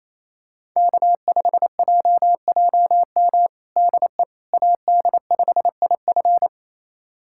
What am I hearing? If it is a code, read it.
Morse code, heard as a keyed tone, about 28 wpm, 700 Hz. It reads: K5JJM DE AD5IF
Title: USA Callsign DE Callsign - 5 Characters - 28wpm